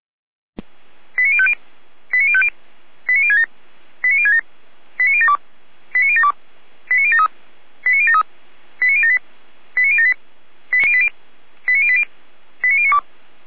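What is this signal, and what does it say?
Five-tone selective call (ZVEI), heard as quick runs of five tones, tones of about 70 ms. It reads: 89050 89050 89067 89067 89072 89072 89073 89073 89078 89078 89080 89092
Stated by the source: Five-tone ZVEI selective call 89020, received via SDR.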